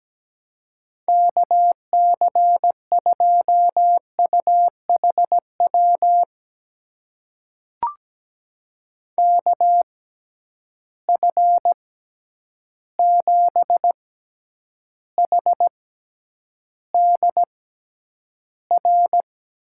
Morse code, heard as a keyed tone, about 17 wpm, spaced as for 9 wpm, 700 Hz. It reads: KC2UHW KF7HDR
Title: USA Call Signs - 6 Characters - 6x Character Spacing - 17wpm